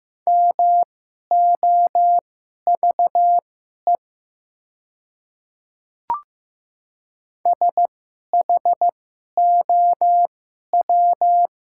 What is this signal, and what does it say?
Morse code, heard as a keyed tone, about 15 wpm, spaced as for 11 wpm, 700 Hz. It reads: MOVE SHOW
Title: 2x Character Spacing - Top 200 Words - 15wpm